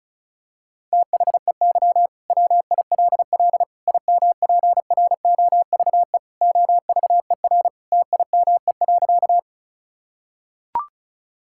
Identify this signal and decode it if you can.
Morse code, heard as a keyed tone, about 34 wpm, 700 Hz. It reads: THEY WILL IMPROVE OVER TIME.
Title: Sentences from Top 1000 Words - Review Part 5 - 35wpm